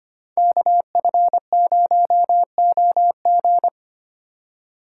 Morse code, heard as a keyed tone, about 33 wpm, 700 Hz. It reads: KF0OG